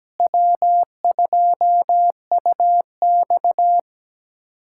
Morse code, heard as a keyed tone, about 17 wpm, 700 Hz. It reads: W2UX
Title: USA Callsign DE Callsign - 4 Characters - 17wpm